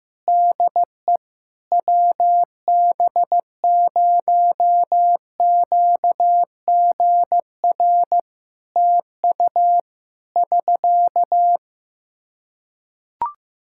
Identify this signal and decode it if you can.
Morse code, heard as a keyed tone, about 15 wpm, 700 Hz. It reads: DE WB0QGR TU <SK>
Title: Practice QSO - Exchange 4 - Ending - 15wpm